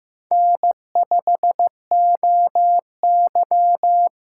Morse code, heard as a keyed tone, about 15 wpm, 700 Hz. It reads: N5OY
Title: USA Call Signs - 4 Characters - 2x Character Spacing - 15wpm